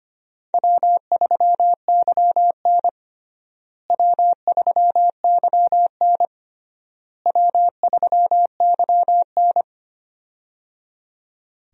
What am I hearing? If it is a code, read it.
Morse code, heard as a keyed tone, about 25 wpm, 700 Hz. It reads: W3YN W3YN W3YN
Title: USA Call Signs - 4 Characters - Repeated 3x - 25wpm